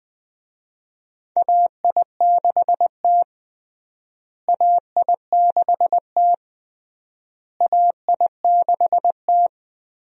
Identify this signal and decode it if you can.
Morse code, heard as a keyed tone, about 20 wpm, 700 Hz. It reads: AI6T AI6T AI6T
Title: USA Call Signs - 4 Characters - Repeated 3x - 20wpm